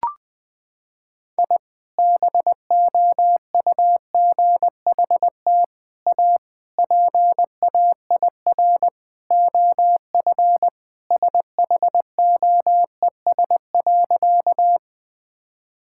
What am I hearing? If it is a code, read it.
Morse code, heard as a keyed tone, about 20 wpm, 700 Hz. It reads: I BOUGHT A PAIR OF SHOES.
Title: Sentences with 3-Word Phrases - 20wpm